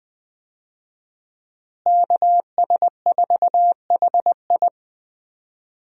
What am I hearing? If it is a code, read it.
Morse code, heard as a keyed tone, about 21 wpm, 700 Hz. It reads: KS4HI
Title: USA Call Signs - 5 Characters - Repeated 3x - 20wpm